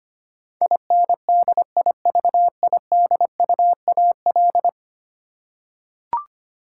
Morse code, heard as a keyed tone, about 25 wpm, 700 Hz. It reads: INDIVIDUAL
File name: Ten Letter Words - 25wpm